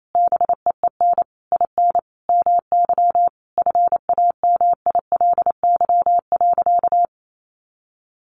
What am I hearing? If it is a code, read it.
Morse code, heard as a keyed tone, about 28 wpm, 700 Hz. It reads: BEEN IN MY FAMILY.
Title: Sentences from Top 1200 Words 28wpm